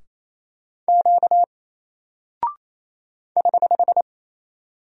Morse code, heard as a keyed tone, about 28 wpm, 700 Hz. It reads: Q <HH>
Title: Single Character (Letter:Number:Punctuation:Prosign) Alt 28wpm